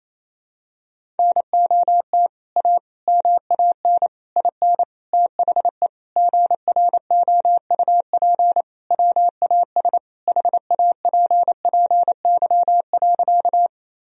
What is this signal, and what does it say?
Morse code, heard as a keyed tone, about 28 wpm, 700 Hz. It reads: NOT A MAN IN THE GROUP WAS HAPPY.
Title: Sentences from Top 1000 Words - Review Part 1 - 28wpm